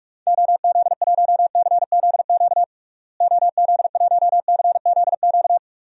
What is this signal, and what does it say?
Morse code, heard as a keyed tone, about 43 wpm, 700 Hz. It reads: OZ1CZQ OZ1CZQ
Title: DXCC-100 Call Signs - 6 Characters - Repeated 2x - 45wpm